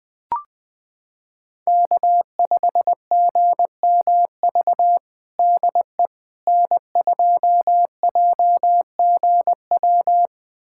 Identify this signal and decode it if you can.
Morse code, heard as a keyed tone, about 20 wpm, 700 Hz. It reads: K5GMV DE N2JGW
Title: USA Callsign DE Callsign - 5 Characters - 20wpm